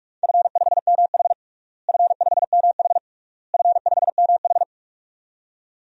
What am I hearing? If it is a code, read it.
Morse code, heard as a keyed tone, about 46 wpm, 700 Hz. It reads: F5MH F5MH F5MH
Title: DXCC-100 Call Signs - 4 Characters - Repeated 3x - 45wpm